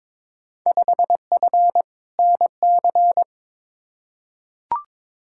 Morse code, heard as a keyed tone, about 22 wpm, 700 Hz. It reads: 5F NC
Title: Field Day Exchanges Encore - 22wpm